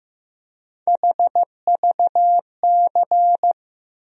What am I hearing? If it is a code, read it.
Morse code, heard as a keyed tone, about 15 wpm, 700 Hz. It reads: HVC
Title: USA Call Sign Postfixes - 3 Letters 15wpm